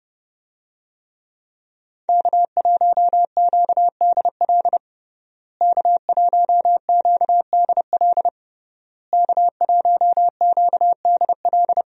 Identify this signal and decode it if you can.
Morse code, heard as a keyed tone, about 30 wpm, 700 Hz. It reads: K1QDL K1QDL K1QDL
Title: USA Call Signs - 5 Characters - Repeated 3x - 30wpm